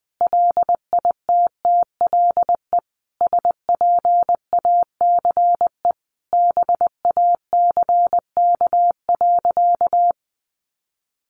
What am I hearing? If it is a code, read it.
Morse code, heard as a keyed tone, about 20 wpm, 700 Hz. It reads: LITTLE SPACE BACK.
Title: Sentences from Top 1500 Words 20wpm